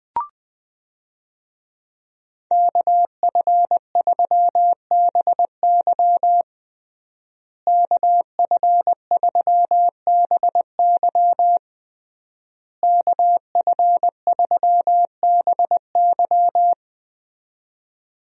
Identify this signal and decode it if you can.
Morse code, heard as a keyed tone, about 21 wpm, 700 Hz. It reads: KF3BY KF3BY KF3BY
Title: USA Call Signs - 5 Characters - Repeated 3x - 20wpm